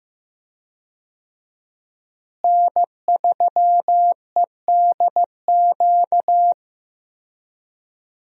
Morse code, heard as a keyed tone, about 15 wpm, 700 Hz. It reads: N3EDQ
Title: USA Call Signs - 5 Characters - Repeated 2x - 15wpm